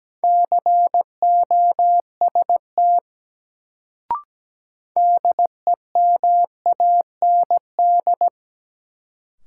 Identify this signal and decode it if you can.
Morse code, heard as a keyed tone, about 17 wpm, 700 Hz. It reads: COST DEMAND
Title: Sentences from Top 800 Words - New Word Drill - 17wpm